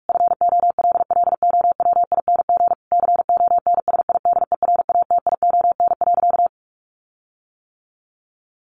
Morse code, heard as a keyed tone, about 44 wpm, 700 Hz. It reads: FOLLOWING CONSIDERATION.
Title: Sentences from Top 1500 Words 45wpm